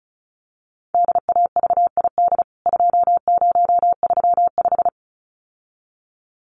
Morse code, heard as a keyed tone, about 34 wpm, 700 Hz. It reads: DAVID 2035
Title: CWT Exchange 35wpm